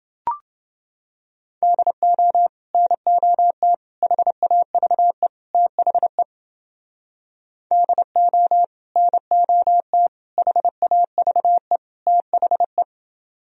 Morse code, heard as a keyed tone, about 30 wpm, 700 Hz. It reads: DO NOT HAVE THE DO NOT HAVE THE